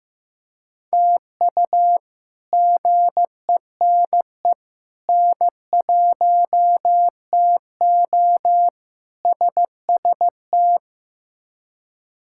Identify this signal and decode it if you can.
Morse code, heard as a keyed tone, about 15 wpm, 700 Hz. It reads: TU GENE N1TO SST
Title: K1USN SST QSO Practice - 15wpm